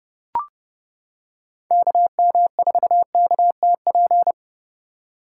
Morse code, heard as a keyed tone, about 29 wpm, 700 Hz. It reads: KM4KTP